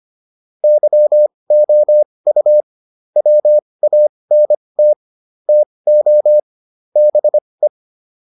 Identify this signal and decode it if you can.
Morse code, heard as a keyed tone, about 25 wpm, spaced as for 20 wpm, 600 Hz. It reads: YOU WANT TO BE